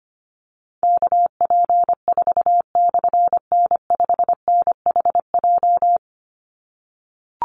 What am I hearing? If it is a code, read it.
Morse code, heard as a keyed tone, about 25 wpm, 700 Hz. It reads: KP4/N5NHJ
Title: Call Signs - 25wpm